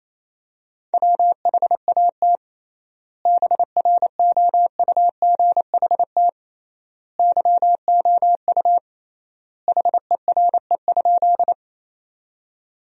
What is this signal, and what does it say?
Morse code, heard as a keyed tone, about 28 wpm, 700 Hz. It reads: WHAT BROUGHT YOU HERE?
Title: 3x Word-Spacing - Sentences from Top 500 Words - 28wpm